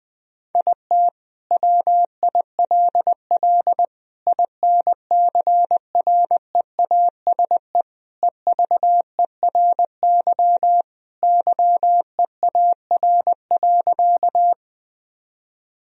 Morse code, heard as a keyed tone, about 19 wpm, 700 Hz. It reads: IT WILL INCREASE EVERY YEAR.
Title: Sentences from Top 1000 Words - Review Part 4 - 20wpm